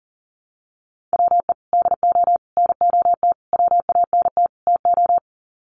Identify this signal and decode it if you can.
Morse code, heard as a keyed tone, about 40 wpm, 700 Hz. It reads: WE DO NOT WANT TO
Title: Sets of 5 Words 40wpm